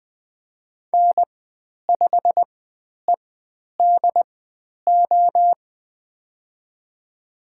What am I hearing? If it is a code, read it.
Morse code, heard as a keyed tone, about 20 wpm, spaced as for 10 wpm, 700 Hz. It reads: N5EDO
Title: 102 Lesson 17g - Letter u - Call Signs 20wpm